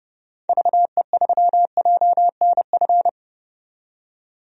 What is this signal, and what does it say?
Morse code, heard as a keyed tone, about 30 wpm, 700 Hz. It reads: VE3JNF
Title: DXCC-100 Call Signs - 6 Characters - 30wpm